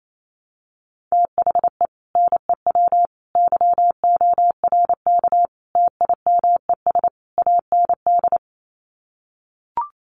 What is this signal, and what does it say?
Morse code, heard as a keyed tone, about 28 wpm, 700 Hz. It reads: THE NEW YORK TIMES AND
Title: Sets of 5 Words Encore 28wpm